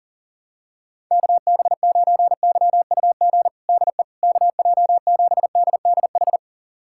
Morse code, heard as a keyed tone, about 40 wpm, 700 Hz. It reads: KB9YUG DE KJ7DDH
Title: USA Callsign DE Callsign - 6 Characters - 40wpm